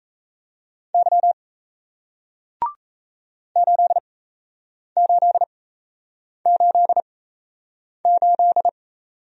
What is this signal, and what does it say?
Morse code, heard as a keyed tone, about 37 wpm, spaced as for 8 wpm, 700 Hz. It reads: Y 8888